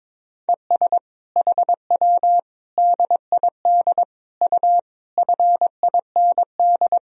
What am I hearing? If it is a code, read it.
Morse code, heard as a keyed tone, about 22 wpm, 700 Hz. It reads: ES HW DID U FIND
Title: Sentences from Top 100 Words and Abbreviations 22wpm